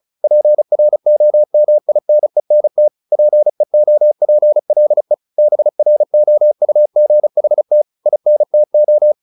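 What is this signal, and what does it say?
Morse code, heard as a keyed tone, about 35 wpm, 600 Hz. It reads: PROMINENT PEOPLE BROUGHT INTO
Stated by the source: Morse practice at 35 wpm from CW med Gnister, sending Ken Follett's novel Never in English.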